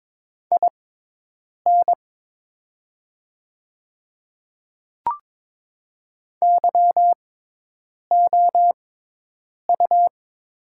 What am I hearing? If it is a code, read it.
Morse code, heard as a keyed tone, about 22 wpm, spaced as for 8 wpm, 700 Hz. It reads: IN YOU
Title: 6x Character Spacing - Top 100 Words - 22wpm